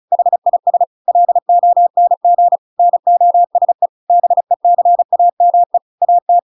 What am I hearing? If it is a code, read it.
Morse code, heard as a keyed tone, about 34 wpm, 700 Hz. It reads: HIS LONG NOSE BECAME AT
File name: Pinocchio / Pinocchio ch5 35 35 ik1hge